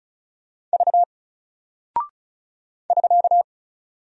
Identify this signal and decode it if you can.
Morse code, heard as a keyed tone, about 36 wpm, 700 Hz. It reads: V <SK>
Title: Single Character (Letter:Number:Punctuation:Prosign) Alt 35wpm